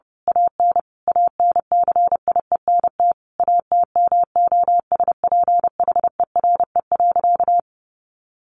Morse code, heard as a keyed tone, about 28 wpm, 700 Hz. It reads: AN ANCIENT ATMOSPHERE.
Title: A Sound Of Thunder 30wpm